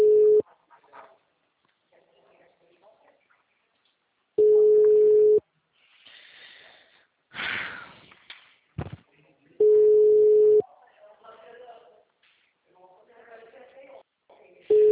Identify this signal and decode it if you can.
Ringback tone (Europe, 425 Hz), with speech.